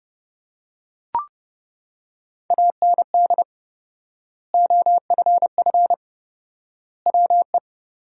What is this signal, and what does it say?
Morse code, heard as a keyed tone, about 32 wpm, 700 Hz. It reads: AND OFF WE
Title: 4x Word-Spacing - Sentences from Top 200 Words - 30wpm